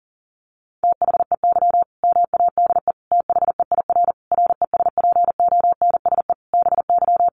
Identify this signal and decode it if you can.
Morse code, heard as a keyed tone, about 40 wpm, 700 Hz. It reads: THEY MADE THEIR RESPONSE BY
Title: Sentences from Top 1400 Words 40wpm